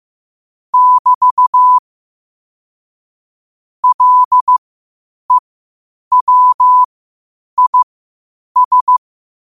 Morse code, heard as a keyed tone, about 15 wpm, spaced as for 8 wpm, 1000 Hz. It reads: <BT> LEWIS
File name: qotdfiles / 8wpm